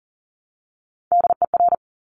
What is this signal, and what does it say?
Morse code, heard as a keyed tone, about 38 wpm, 700 Hz. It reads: DER
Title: Top Three Letter Combinations in Words 40wpm